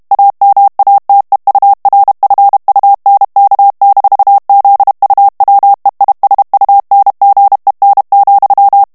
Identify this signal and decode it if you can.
Morse code, heard as a keyed tone, about 32 wpm, 800 Hz. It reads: AMATEURFUNK-ZUWEISUNGEN,
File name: ALLEMAND texte clair 1 vitesse 30 mots minute